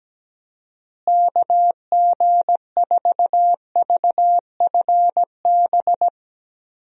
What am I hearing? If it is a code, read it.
Morse code, heard as a keyed tone, about 17 wpm, 700 Hz. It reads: KG4VFB
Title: USA Call Signs - 6 Characters - Repeated 2x - 17wpm